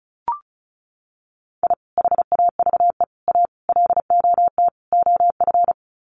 Morse code, heard as a keyed tone, about 34 wpm, 700 Hz. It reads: I HAVE A LOT OF
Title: Sets of 5 Words 35wpm